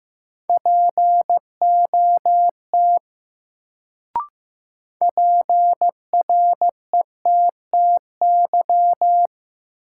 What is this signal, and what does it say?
Morse code, heard as a keyed tone, about 15 wpm, 700 Hz. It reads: POT PRETTY
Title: Sentences from Top 1000 Words - New Word Drill - 15wpm